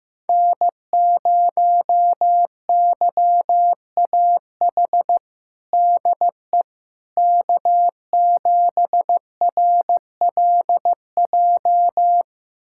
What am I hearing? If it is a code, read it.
Morse code, heard as a keyed tone, about 15 wpm, 700 Hz. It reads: N0YAH DE K7RLJ
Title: USA Callsign DE Callsign - 5 Characters - 15wpm